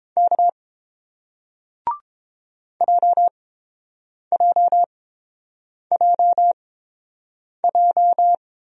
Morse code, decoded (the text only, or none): K JJJJ